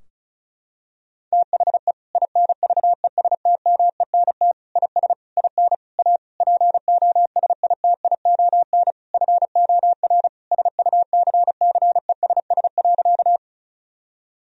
Morse code, heard as a keyed tone, about 34 wpm, 700 Hz. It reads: THE INVESTMENT IS IN A POSITION FOR SUCCESS.
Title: Sentences from Top 1000 Words - Review Part 4 - 35wpm